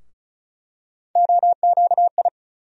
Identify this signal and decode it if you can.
Morse code, heard as a keyed tone, about 34 wpm, 700 Hz. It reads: OQI
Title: USA Call Sign Postfixes - 3 Letters 35wpm